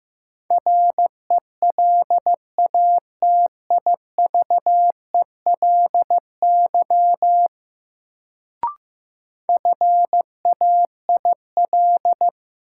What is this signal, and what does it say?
Morse code, heard as a keyed tone, about 15 wpm, 700 Hz. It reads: RELATIVELY FAIL
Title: Sentences from Top 1100 Words - New Word Drill - 15wpm